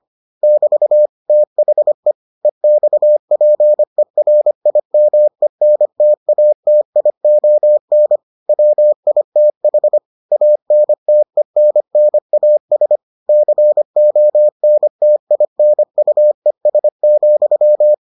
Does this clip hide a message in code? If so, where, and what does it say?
Yes: Morse code from 0.4 to 17.9 s, heard as a keyed tone, about 25 wpm, 600 Hz. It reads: <BT> THE EXPERIMENTATION WITH ANTENNAS CONTINUES,